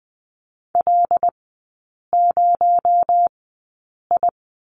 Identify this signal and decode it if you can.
Morse code, heard as a keyed tone, about 20 wpm, spaced as for 8 wpm, 700 Hz. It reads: L0I
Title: 218 Lesson 35e - Number 0 - Three Characters 20wpm